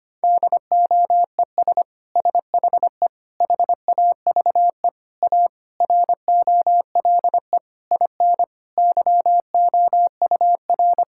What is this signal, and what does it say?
Morse code, heard as a keyed tone, about 25 wpm, 700 Hz. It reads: DOES SHE HAVE A ROLE IN YOUR
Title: Sentences from Top 1000 Words - Review Part 6 - 25wpm